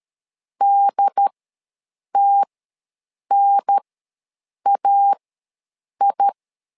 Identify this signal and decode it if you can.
Morse code, heard as a keyed tone, about 13 wpm, spaced as for 7 wpm, 800 Hz. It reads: DTNAI
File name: LEZIONI / 3 MWRO / 3 MWR ws 12wpm 1809